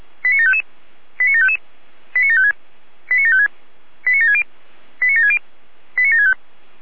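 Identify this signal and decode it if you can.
Five-tone selective call (ZVEI), heard as quick runs of five tones, tones of about 70 ms. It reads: 89755 89755 89756 89756 89760 89760 89765
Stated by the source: ZVEI five-tone call 89706 received on SDR.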